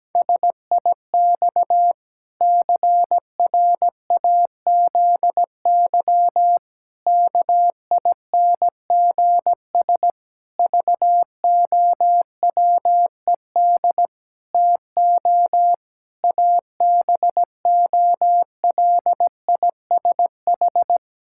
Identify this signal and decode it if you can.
Morse code, heard as a keyed tone, about 17 wpm, 700 Hz. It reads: SIX CRAZY KINGS VOWED TO ABOLISH
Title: Pangram Challenge v2 - 17wpm